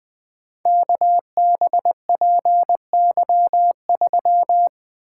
Morse code, heard as a keyed tone, about 20 wpm, 700 Hz. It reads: KBPY3
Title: Code Groups - 5 Letter:Numbers - 20wpm